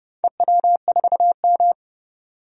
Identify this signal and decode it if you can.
Morse code, heard as a keyed tone, about 31 wpm, 700 Hz. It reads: EW4M
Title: DXCC-100 Call Signs - 4 Characters - 30wpm